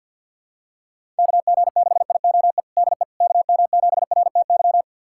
Morse code, heard as a keyed tone, about 49 wpm, 700 Hz. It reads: KG6IOE DE KM7RTY
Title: USA Callsign DE Callsign 50wpm